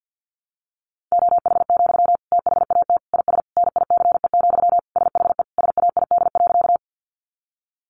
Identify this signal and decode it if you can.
Morse code, heard as a keyed tone, about 48 wpm, 700 Hz. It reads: OH, THAT IS NICE, SHE SAID.
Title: Sentences from Top 1100 Words 50wpm